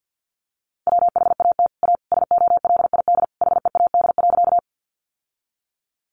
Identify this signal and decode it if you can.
Morse code, heard as a keyed tone, about 50 wpm, 700 Hz. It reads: WHAT A SOLID HEAD.
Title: Sentences from Top 1500 Words 50wpm